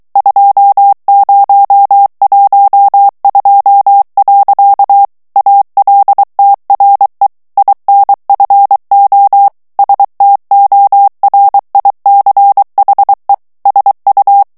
Morse code, heard as a keyed tone, about 23 wpm, 800 Hz. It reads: 2012. ALTRE INFO STORICHE SU
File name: ITALIEN texte clair 1 vitesse 22 mots minute